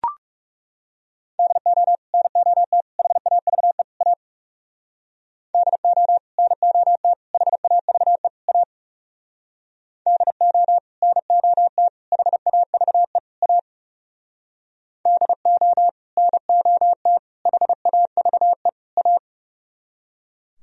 Morse code, heard as a keyed tone, about 36 wpm, 700 Hz. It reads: DO NOT HAVE A DO NOT HAVE A DO NOT HAVE A DO NOT HAVE A
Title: Speed-Racing - Sets of 4 Words 30wpm